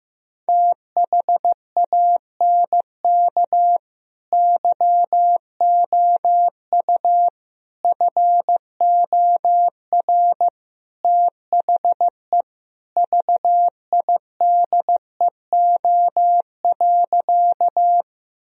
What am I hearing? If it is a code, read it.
Morse code, heard as a keyed tone, about 16 wpm, 700 Hz. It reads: THANK YOU FOR THE VIDEO.
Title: Sentences from Top 1000 Words - Review Part 4 - 15wpm